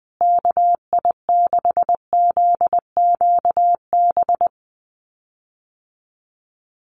Morse code, heard as a keyed tone, about 19 wpm, 700 Hz. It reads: KI6ZQB
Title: USA Call Signs Repeated 2x 20wpm